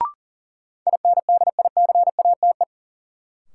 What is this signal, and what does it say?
Morse code, heard as a keyed tone, about 40 wpm, 700 Hz. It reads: INDICATE
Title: Sentences from Top 1300 Words - New Word Drill - 40wpm